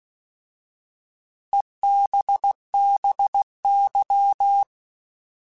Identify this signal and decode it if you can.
Morse code, heard as a keyed tone, about 16 wpm, 800 Hz. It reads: EBBY
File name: Alphabetical / Alphabetical mp3s / Es @15